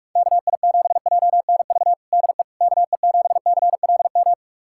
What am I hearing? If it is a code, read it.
Morse code, heard as a keyed tone, about 46 wpm, 700 Hz. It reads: KI7JNV DE KE7CLM
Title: USA Callsign DE Callsign - 6 Characters - 45wpm